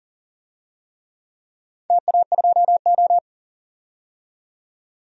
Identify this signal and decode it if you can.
Morse code, heard as a keyed tone, about 40 wpm, 700 Hz. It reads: TA2O
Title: DXCC-100 Call Signs - 4 Characters - 40wpm